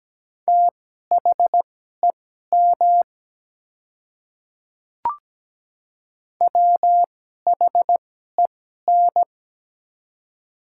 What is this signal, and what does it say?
Morse code, heard as a keyed tone, about 17 wpm, spaced as for 12 wpm, 700 Hz. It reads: THEM WHEN
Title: 2x Character Spacing - Top 100 Words - 17wpm